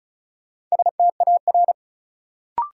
Morse code, heard as a keyed tone, about 36 wpm, 700 Hz. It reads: STAR
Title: Sentences from Top 1000 Words - Review Part 2 - New Word Drill - 35wpm